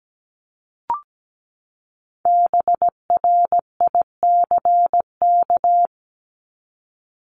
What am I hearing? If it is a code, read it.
Morse code, heard as a keyed tone, about 17 wpm, 700 Hz. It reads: BRICK
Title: Top 1751-2000 Words 17wpm